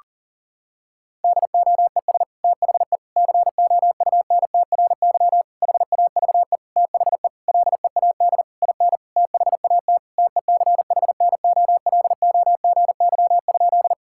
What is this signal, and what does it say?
Morse code, heard as a keyed tone, about 37 wpm, 700 Hz. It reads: DOES THE COUNTRY HAVE THE LEAD IN THAT TECHNOLOGY?
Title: Sentences from Top 1000 Words - Review Part 5 - 40wpm